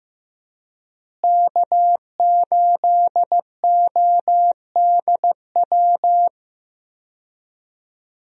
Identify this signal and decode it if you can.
Morse code, heard as a keyed tone, about 15 wpm, 700 Hz. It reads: K8ODW
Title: USA Call Signs - 5 Characters - Repeated 3x - 15wpm